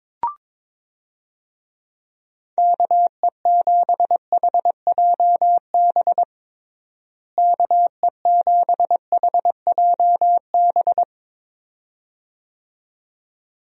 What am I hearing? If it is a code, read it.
Morse code, heard as a keyed tone, about 22 wpm, 700 Hz. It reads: KE7HJB KE7HJB
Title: USA Call Signs - 6 Characters - Repeated 2x - 22wpm